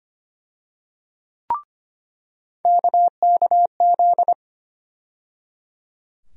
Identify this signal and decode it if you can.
Morse code, heard as a keyed tone, about 25 wpm, 700 Hz. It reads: KKZ
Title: USA Call Sign Postfixes - 3 Letters 25wpm